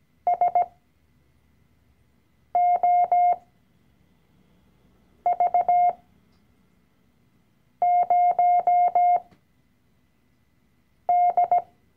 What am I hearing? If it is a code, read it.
Morse code, heard as a keyed tone, about 17 wpm, spaced as for 4 wpm, 710 Hz. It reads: SOV0D
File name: CW-Online-Kurs / Anfänger / Lektion 012